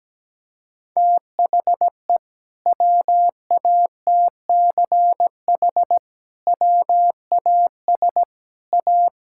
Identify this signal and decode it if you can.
Morse code, heard as a keyed tone, about 17 wpm, 700 Hz. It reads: THE WATCH WAS A
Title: Sentences from Top 1000 Words - Review Part 6 - 17wpm